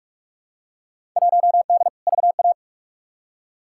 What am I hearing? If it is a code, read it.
Morse code, heard as a keyed tone, about 46 wpm, 700 Hz. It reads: 1D VA
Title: Field Day Exchanges - 45wpm